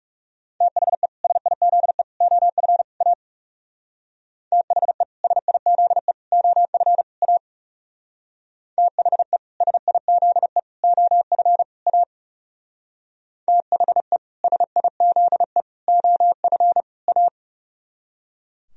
Morse code, heard as a keyed tone, about 39 wpm, 700 Hz. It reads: THE SIZE OF A THE SIZE OF A THE SIZE OF A THE SIZE OF A